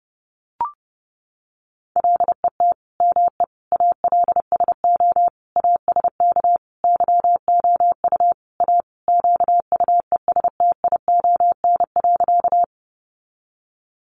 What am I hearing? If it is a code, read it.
Morse code, heard as a keyed tone, about 30 wpm, 700 Hz. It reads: LET ME ALSO ASK YOU A QUESTION.
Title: Sentences from Top 400 Words 30wpm